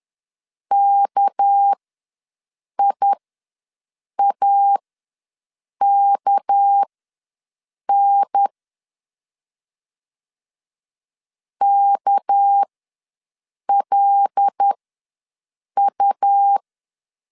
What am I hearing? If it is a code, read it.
Morse code, heard as a keyed tone, about 10 wpm, spaced as for 6 wpm, 800 Hz. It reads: KIAKN KLU
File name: LEZIONI / 6 PKX0 / 6 PK ws 10wpm 3745